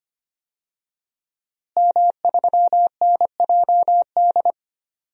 Morse code, heard as a keyed tone, about 25 wpm, 700 Hz. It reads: M3NJD
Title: DXCC-100 Call Signs - 5 Characters - Repeated 2x - 25wpm